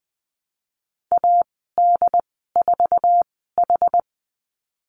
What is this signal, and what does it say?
Morse code, heard as a keyed tone, about 20 wpm, spaced as for 14 wpm, 700 Hz. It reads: AD4H
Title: 069 Lesson 12g - Letter d - Call Signs 20wpm